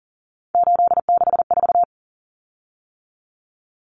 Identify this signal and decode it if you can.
Morse code, heard as a keyed tone, about 40 wpm, 700 Hz. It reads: 864
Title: Common Cut Numbers - 3-Digit 40wpm